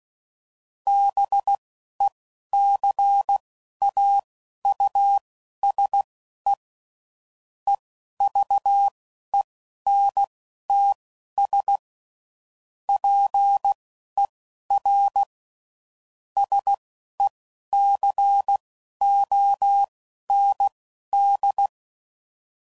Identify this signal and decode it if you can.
Morse code, heard as a keyed tone, about 16 wpm, spaced as for 11 wpm, 800 Hz. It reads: BECAUSE EVENTS PER SECOND
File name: CW Code Groups @10WPM / Technical Subjects / XC